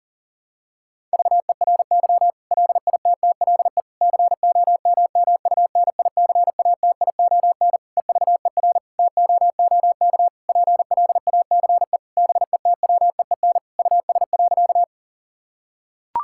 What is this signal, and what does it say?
Morse code, heard as a keyed tone, about 37 wpm, 700 Hz. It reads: VERY LITTLE COMMUNICATION EVER TOOK PLACE BETWEEN US.
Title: Sentences from Top 1000 Words - Review Part 6 - 40wpm